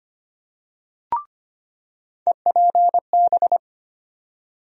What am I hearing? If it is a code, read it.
Morse code, heard as a keyed tone, about 25 wpm, 700 Hz. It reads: EPB
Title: USA Call Sign Postfixes - 3 Letters 25wpm